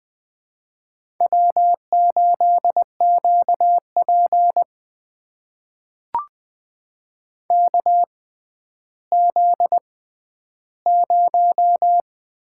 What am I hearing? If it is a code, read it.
Morse code, heard as a keyed tone, about 20 wpm, spaced as for 11 wpm, 700 Hz. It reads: W8QP KZ0
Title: USA Call Signs - 4 Characters - 6x Character Spacing - 20wpm